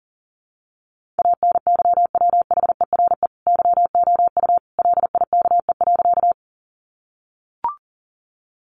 Morse code, heard as a keyed tone, about 40 wpm, 700 Hz. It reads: ANYWHERE YOU LIKE.
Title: Sentences from Top 1400 Words 40wpm